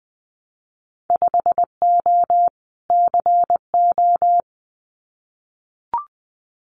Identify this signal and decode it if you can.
Morse code, heard as a keyed tone, about 20 wpm, 700 Hz. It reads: 5O CO
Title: Winter Field Day Exchanges Encore 20wpm